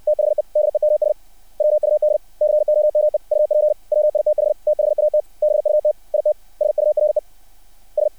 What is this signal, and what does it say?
Morse code, heard as a keyed tone, about 60 wpm, 600 Hz. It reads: THE DEMS WUD COME UP WITH THAT BUT IT SURE S
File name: archived / CRSnet